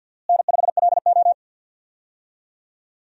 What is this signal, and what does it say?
Morse code, heard as a keyed tone, about 52 wpm, 700 Hz. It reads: N5LO
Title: USA Call Signs - 4 Characters - Repeated 2x - 50wpm